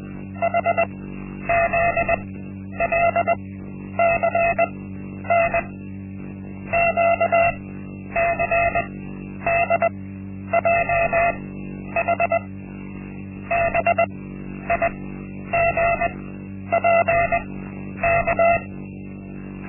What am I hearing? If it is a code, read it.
Morse code, heard as a keyed tone, about 20 wpm, spaced as for 10 wpm, 680 Hz. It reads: HZLCN QCDJH BIGPK
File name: Linksmakalnis-1